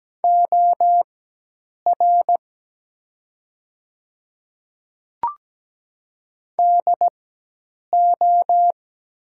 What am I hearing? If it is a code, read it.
Morse code, heard as a keyed tone, about 17 wpm, spaced as for 8 wpm, 700 Hz. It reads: OR DO
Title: 4x Character Spacing - Top 200 Words - 17wpm